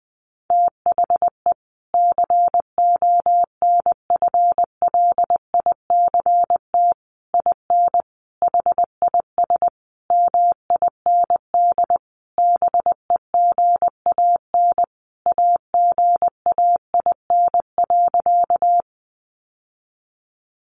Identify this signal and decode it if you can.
Morse code, heard as a keyed tone, about 20 wpm, 700 Hz. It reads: THE CONFLICT IN HIS MIND BEGAN AGAIN.
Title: Sentences from Top 1300 Words 20wpm